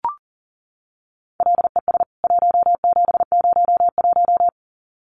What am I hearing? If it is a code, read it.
Morse code, heard as a keyed tone, about 40 wpm, 700 Hz. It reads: LES 1701
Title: CWT Exchange 40wpm